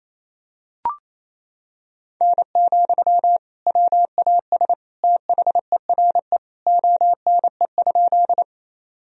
Morse code, heard as a keyed tone, about 28 wpm, 700 Hz. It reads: N, WAS THERE ONE?
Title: Sentences from Top 100 Words and Abbreviations 28wpm